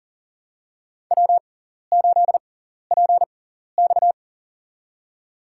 Morse code, heard as a keyed tone, about 40 wpm, spaced as for 14 wpm, 700 Hz. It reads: W8PX
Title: USA Call Signs - 4 Characters - 6x Character Spacing - 40wpm